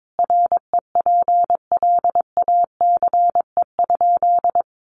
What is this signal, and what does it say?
Morse code, heard as a keyed tone, about 22 wpm, 700 Hz. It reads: REPLACE?
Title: Sentences from Top 1400 Words 22wpm